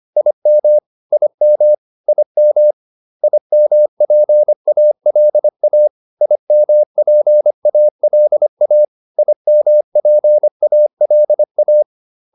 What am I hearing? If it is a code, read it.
Morse code, heard as a keyed tone, about 25 wpm, 600 Hz. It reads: IM IM IM IMPALA IMPALA IMPALA